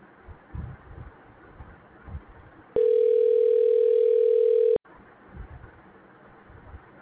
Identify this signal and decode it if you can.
Ringback tone (North America), 440 + 480 Hz.